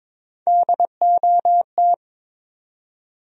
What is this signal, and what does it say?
Morse code, heard as a keyed tone, about 22 wpm, 700 Hz. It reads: DOT
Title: USA Common Names - Max 4 Letters - 22wpm